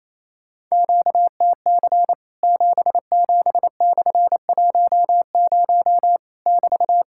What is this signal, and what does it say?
Morse code, heard as a keyed tone, about 28 wpm, 700 Hz. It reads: QTC 77/10 <BT>
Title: Worked All Europe DX Contest - QTC Exchange - 28wpm